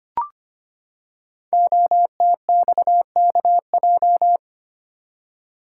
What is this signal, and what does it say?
Morse code, heard as a keyed tone, about 25 wpm, 700 Hz. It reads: OTXKJ
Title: Code Groups - 5 Letter:Numbers - 25wpm